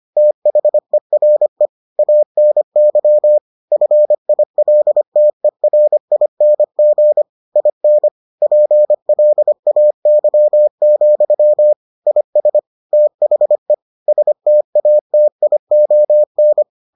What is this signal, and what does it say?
Morse code, heard as a keyed tone, about 25 wpm, 600 Hz. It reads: THERE ANY FILTERING IN PLAY, IS THE STATION